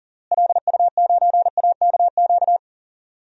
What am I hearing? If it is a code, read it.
Morse code, heard as a keyed tone, about 40 wpm, 700 Hz. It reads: LU9AKQ
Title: DXCC-100 Call Signs - 6 Characters - Repeated 3x - 40wpm